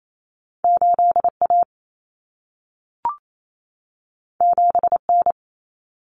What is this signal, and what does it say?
Morse code, heard as a keyed tone, about 28 wpm, 700 Hz. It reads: 8A 7N
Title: Common Cut Numbers - 2-Digit 28wpm